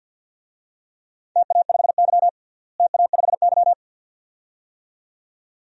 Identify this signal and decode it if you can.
Morse code, heard as a keyed tone, about 52 wpm, 700 Hz. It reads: TA5Y TA5Y
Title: DXCC-100 Call Signs - 4 Characters - Repeated 2x - 50wpm